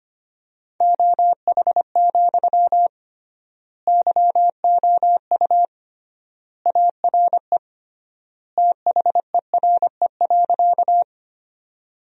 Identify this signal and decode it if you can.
Morse code, heard as a keyed tone, about 25 wpm, 700 Hz. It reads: OH, YOU ARE THERE.
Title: 3x Word-Spacing - Sentences from Top 500 Words - 25wpm